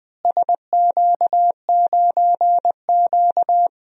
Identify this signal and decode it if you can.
Morse code, heard as a keyed tone, about 20 wpm, 700 Hz. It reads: SQ9Q
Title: DXCC-100 Call Signs - 4 Characters - Repeated 2x - 20wpm